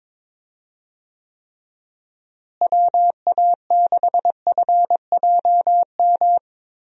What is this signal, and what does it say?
Morse code, heard as a keyed tone, about 22 wpm, 700 Hz. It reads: WA6FJM
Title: USA Call Signs - 6 Characters - Repeated 2x - 22wpm